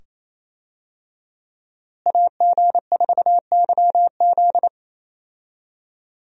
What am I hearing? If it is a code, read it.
Morse code, heard as a keyed tone, about 28 wpm, 700 Hz. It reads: AG4YZ
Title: USA Call Signs - 5 Characters - Repeated 2x - 28wpm